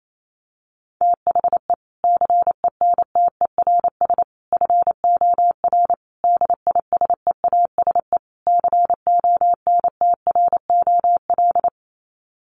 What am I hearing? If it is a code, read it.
Morse code, heard as a keyed tone, about 28 wpm, 700 Hz. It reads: THE CENTERS FOR DISEASE CONTROL
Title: Sets of 5 Words Encore2 28wpm